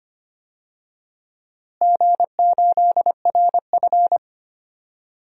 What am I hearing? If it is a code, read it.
Morse code, heard as a keyed tone, about 25 wpm, 700 Hz. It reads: G8RF
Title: DXCC-100 Call Signs - 4 Characters - Repeated 3x - 25wpm